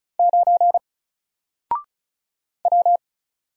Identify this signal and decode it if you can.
Morse code, heard as a keyed tone, about 36 wpm, 700 Hz. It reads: 9 W
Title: Single Letter-Number Alt 35wpm